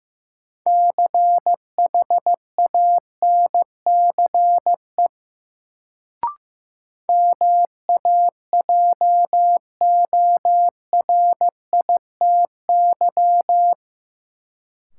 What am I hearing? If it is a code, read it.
Morse code, heard as a keyed tone, about 15 wpm, 700 Hz. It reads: CHANCE MAJORITY
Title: Sets of 3 Words - New Word Drill - 15wpm